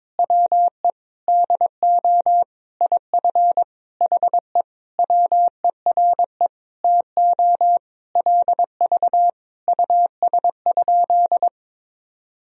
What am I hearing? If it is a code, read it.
Morse code, heard as a keyed tone, about 22 wpm, 700 Hz. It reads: WE DO IF HE WERE TO LV US?
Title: Sentences from Top 400 Words and Abbreviations 22wpm